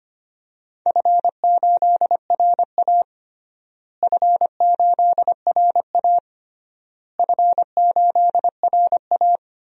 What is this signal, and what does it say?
Morse code, heard as a keyed tone, about 25 wpm, 700 Hz. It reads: F8RA F8RA F8RA
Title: DXCC-100 Call Signs - 4 Characters - Repeated 3x - 25wpm